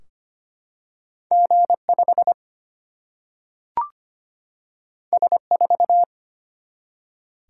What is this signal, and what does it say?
Morse code, heard as a keyed tone, about 25 wpm, 700 Hz. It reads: G5 S4
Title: Single-Letter Single-Number 25wpm